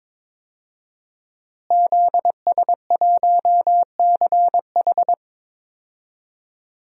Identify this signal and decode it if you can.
Morse code, heard as a keyed tone, about 21 wpm, 700 Hz. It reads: ZS1CH
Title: DXCC-100 Call Signs - 5 Characters - Repeated 3x - 22wpm